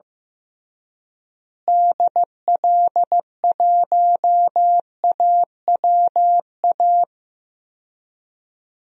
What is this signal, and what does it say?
Morse code, heard as a keyed tone, about 15 wpm, 700 Hz. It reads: DL1AWA
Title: DXCC-100 Call Signs - 6 Characters - Repeated 3x - 15wpm